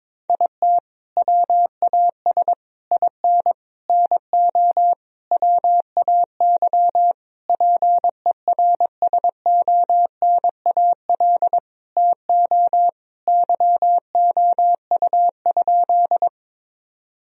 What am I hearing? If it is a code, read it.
Morse code, heard as a keyed tone, about 22 wpm, 700 Hz. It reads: IT WAS IN NO WAY PERSONAL TO YOU?
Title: Sentences from Top 1000 Words - Review Part 4 - 22wpm